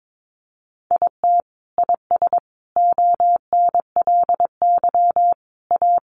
Morse code, heard as a keyed tone, about 22 wpm, 700 Hz. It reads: IT IS ONLY A
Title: Sentences from Top 700 Words 22wpm